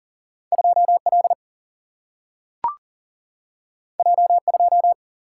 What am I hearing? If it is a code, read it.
Morse code, heard as a keyed tone, about 40 wpm, 700 Hz. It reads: 2L J2
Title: Single-Letter Single-Number 40wpm